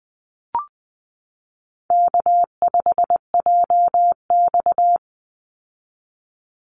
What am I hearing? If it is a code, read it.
Morse code, heard as a keyed tone, about 20 wpm, 700 Hz. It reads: K5JX
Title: Easy Call Signs Repeated 2x - 20wpm